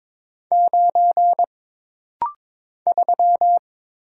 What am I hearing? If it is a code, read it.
Morse code, heard as a keyed tone, about 22 wpm, 700 Hz. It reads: 9 3
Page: This rapid-fire timing is geared to push your ICR (Instant Character Recognition) skills.